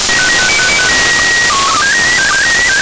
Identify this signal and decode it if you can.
SSTV (slow-scan television), PD 120.
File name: samples ogg / satnogs